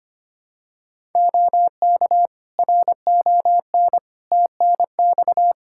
Morse code, heard as a keyed tone, about 25 wpm, 700 Hz. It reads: OK RON TNX
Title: Practice QSO - Exchange 4 - Ending - 25wpm